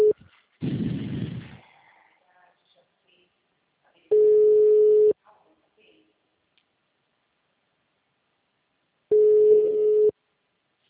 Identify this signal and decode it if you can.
Ringback tone (Europe, 425 Hz), with speech.